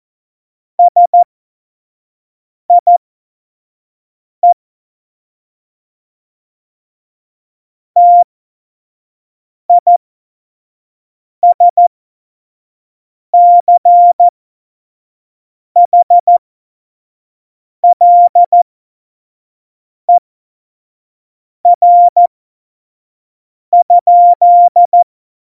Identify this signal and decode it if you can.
Morse code, heard as a keyed tone, about 14 wpm, spaced as for 5 wpm, 700 Hz. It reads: SIE TISCHLER?
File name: Morsekurs / Witze / Witz 15. Oktober 2019, 25 F70